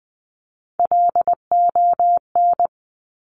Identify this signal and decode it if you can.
Morse code, heard as a keyed tone, about 20 wpm, 700 Hz. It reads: LON
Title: Top Three Letter Combinations in Words 20wpm